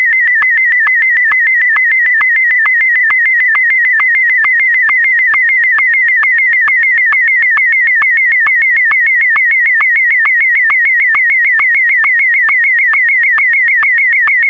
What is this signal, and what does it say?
SSTV (slow-scan television), Martin 1.
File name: sstv / sstv1 / martin m1